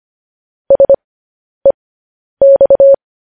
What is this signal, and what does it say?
Morse code, heard as a keyed tone, about 25 wpm, spaced as for 10 wpm, 550 Hz. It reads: SEX
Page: News Headlines in Morse Code at 10 WPM